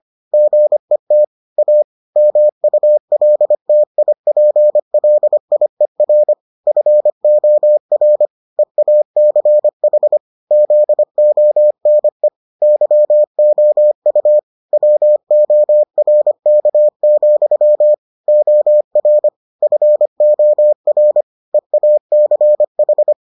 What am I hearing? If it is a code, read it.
Morse code, heard as a keyed tone, about 25 wpm, 600 Hz. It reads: GET A MULTIPLIER FOR EACH ZONE YOU WORK, OR FOR EACH